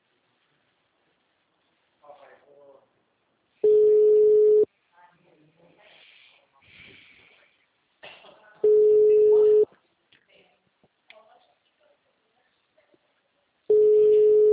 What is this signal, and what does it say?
Ringback tone (Europe, 425 Hz), with speech.
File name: out- -2011- - - .14765